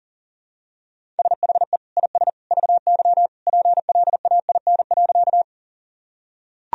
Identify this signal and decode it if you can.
Morse code, heard as a keyed tone, about 40 wpm, 700 Hz. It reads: SHE IS VY PLAIN.
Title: Sentences from Top 400 Words and Abbreviations 40wpm